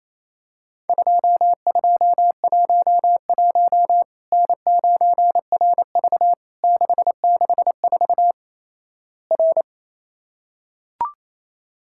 Morse code, heard as a keyed tone, about 27 wpm, 700 Hz. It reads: 2211 N9RV 664 R
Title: Worked All Europe DX Contest - QTC Exchange - 28wpm